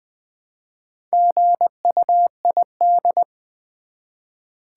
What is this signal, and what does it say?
Morse code, heard as a keyed tone, about 20 wpm, 700 Hz. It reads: GUID
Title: Build A Word - 1 Letter At A Time - 5-Letter Words - 20wpm